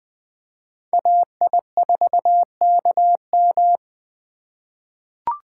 Morse code, heard as a keyed tone, about 20 wpm, 700 Hz. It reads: AI4KM
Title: USA Call Signs - 5 Characters - 4x Character Spacing - 20wpm